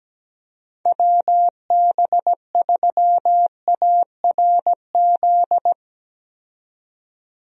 Morse code, heard as a keyed tone, about 17 wpm, 700 Hz. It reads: WB3ARZ
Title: USA Call Signs - 6 Characters - 17wpm